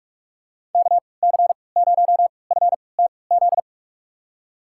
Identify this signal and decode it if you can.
Morse code, heard as a keyed tone, about 46 wpm, spaced as for 26 wpm, 690 Hz. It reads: KC0FTZ
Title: USA Call Signs - 6 Characters - 3x Character Spacing - 45wpm